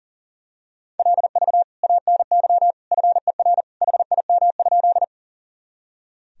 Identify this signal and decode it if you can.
Morse code, heard as a keyed tone, about 40 wpm, 700 Hz. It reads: LV ANY FER HIM?
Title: Sentences from Top 400 Words and Abbreviations 40wpm